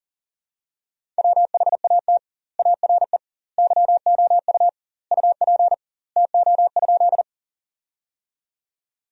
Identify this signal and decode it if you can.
Morse code, heard as a keyed tone, about 40 wpm, 700 Hz. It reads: WHAT ARE YOU UP TO?
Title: 2x Word-Spacing - Sentences from Top 100 Words - 40wpm